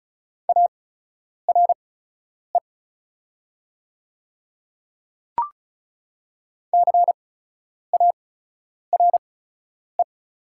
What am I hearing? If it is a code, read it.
Morse code, heard as a keyed tone, about 36 wpm, spaced as for 10 wpm, 690 Hz. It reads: ARE CARE
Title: 8x Character Spacing - Top 500 Words - 35wpm